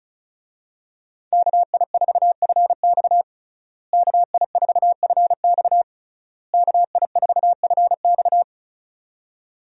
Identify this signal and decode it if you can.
Morse code, heard as a keyed tone, about 36 wpm, 700 Hz. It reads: KI4FX KI4FX KI4FX
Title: USA Call Signs - 5 Characters - Repeated 3x - 35wpm